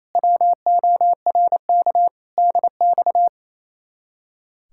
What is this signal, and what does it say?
Morse code, heard as a keyed tone, about 33 wpm, 700 Hz. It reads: WORK DX
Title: Speed-Racing Common QSO Phrases - 28wpm